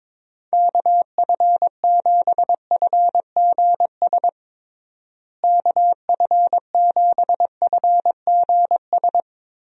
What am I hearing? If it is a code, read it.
Morse code, heard as a keyed tone, about 22 wpm, 700 Hz. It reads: KF7FGS KF7FGS
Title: USA Call Signs - 6 Characters - Repeated 3x - 22wpm